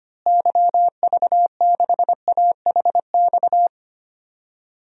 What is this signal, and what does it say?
Morse code, heard as a keyed tone, about 25 wpm, 700 Hz. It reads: YV6AHX
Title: DXCC-100 Call Signs - 6 Characters - Repeated 3x - 25wpm